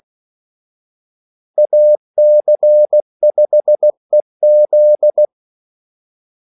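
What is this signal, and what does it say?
Morse code, heard as a keyed tone, about 16 wpm, 600 Hz. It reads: AC5EZ
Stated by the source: Morse call sign practice at 16 wpm.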